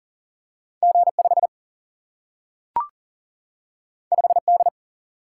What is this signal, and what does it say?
Morse code, heard as a keyed tone, about 40 wpm, 700 Hz. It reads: G5 5D
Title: Single-Letter Single-Number 40wpm